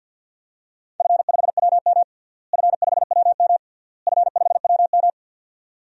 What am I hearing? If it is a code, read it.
Morse code, heard as a keyed tone, about 50 wpm, 700 Hz. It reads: F5WM F5WM F5WM
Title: DXCC-100 Call Signs - 4 Characters - Repeated 3x - 50wpm